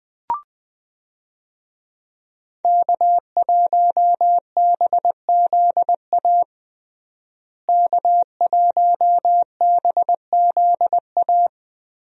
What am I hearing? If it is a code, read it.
Morse code, heard as a keyed tone, about 20 wpm, 700 Hz. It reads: K1BZA K1BZA
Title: USA Call Signs - 5 Characters - Repeated 3x - 20wpm